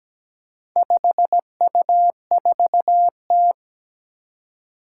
Morse code, heard as a keyed tone, about 17 wpm, 700 Hz. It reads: 5U4T
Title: Four Challenging Letter-Number Combinations 17wpm